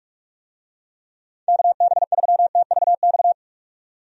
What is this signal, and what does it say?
Morse code, heard as a keyed tone, about 42 wpm, 700 Hz. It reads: KB3TVX
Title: USA Call Signs - 6 Characters - 45wpm